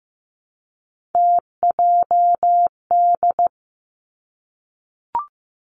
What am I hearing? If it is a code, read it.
Morse code, heard as a keyed tone, about 15 wpm, 700 Hz. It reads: TJD
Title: Three Challenging Letter-Number Combinations 15wpm